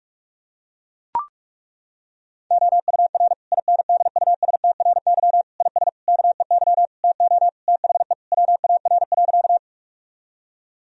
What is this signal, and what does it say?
Morse code, heard as a keyed tone, about 41 wpm, 700 Hz. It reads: OUR INDUSTRY IS KEY TO THE WAR.
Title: Sentences from Top 1000 Words - Review Part 4 - 45wpm